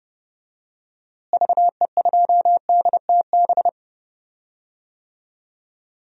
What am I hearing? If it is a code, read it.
Morse code, heard as a keyed tone, about 30 wpm, 700 Hz. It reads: VE2DTB
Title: DXCC-100 Call Signs - 6 Characters - Repeated 3x - 30wpm